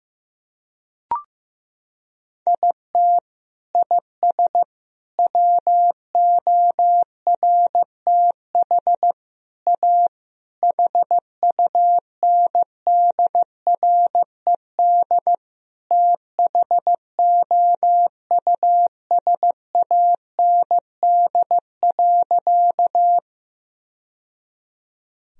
Morse code, heard as a keyed tone, about 16 wpm, 700 Hz. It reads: IT IS WORTH A HUNDRED THOUSAND.
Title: Sentences from Top 1000 Words - Review Part 3 - 15wpm